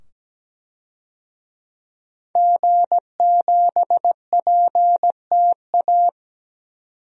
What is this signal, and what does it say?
Morse code, heard as a keyed tone, about 17 wpm, 700 Hz. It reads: G7PTA
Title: DXCC-100 Call Signs - 5 Characters - 17wpm